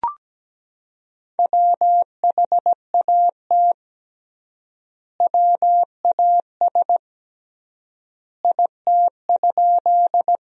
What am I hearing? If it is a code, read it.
Morse code, heard as a keyed tone, about 17 wpm, 700 Hz. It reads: WHAT WAS IT?
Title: 3x Word-Spacing - Sentences from Top 100 Words - 17wpm